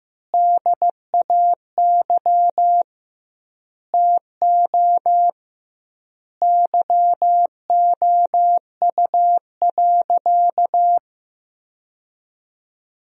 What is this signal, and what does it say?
Morse code, heard as a keyed tone, about 15 wpm, 700 Hz. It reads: DAY TO YOU.
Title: 2x Word-Spacing - Sentences from Top 200 Words - 15wpm